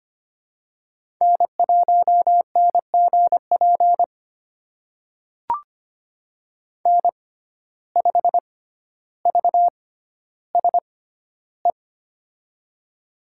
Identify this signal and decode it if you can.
Morse code, heard as a keyed tone, about 25 wpm, spaced as for 13 wpm, 700 Hz. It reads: N1NGP N5VSE
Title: USA Call Signs - 5 Characters - 6x Character Spacing - 25wpm